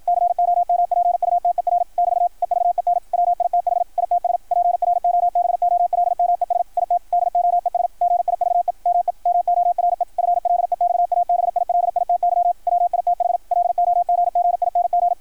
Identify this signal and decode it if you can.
Morse code, heard as a keyed tone, about 53 wpm, 660 Hz, with radio noise. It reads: COMPUTER <BT> EVEN WITH ITS PROBOLMES IT DOES GIVE ME MORE FLEXABILITY WITH LOGGING
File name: archived / CRSnet